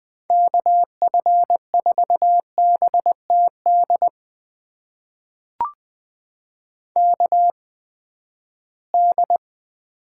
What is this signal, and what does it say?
Morse code, heard as a keyed tone, about 20 wpm, spaced as for 14 wpm, 700 Hz. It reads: KF4BTD KD
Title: USA Call Signs - 6 Characters - 8x Character Spacing - 20wpm